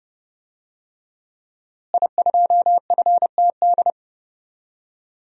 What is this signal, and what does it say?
Morse code, heard as a keyed tone, about 30 wpm, 700 Hz. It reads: I2FTD
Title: DXCC-100 Call Signs - 5 Characters - Repeated 3x - 30wpm